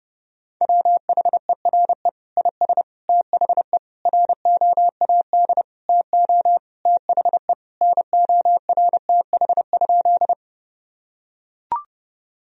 Morse code, heard as a keyed tone, about 30 wpm, 700 Hz. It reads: WHERE IS THE ROAD TO THE NORTH?
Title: Sentences from Top 1000 Words - Review Part 2 - 30wpm